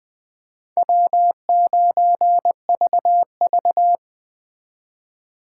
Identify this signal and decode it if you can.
Morse code, heard as a keyed tone, about 20 wpm, 700 Hz. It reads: W9VV